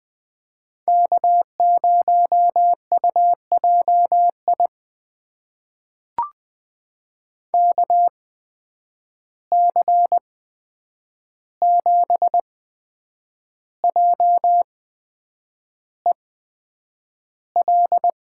Morse code, heard as a keyed tone, about 20 wpm, spaced as for 8 wpm, 700 Hz. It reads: K0UJI KC7JEL
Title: USA Call Signs - 8x Character Spacing - 20wpm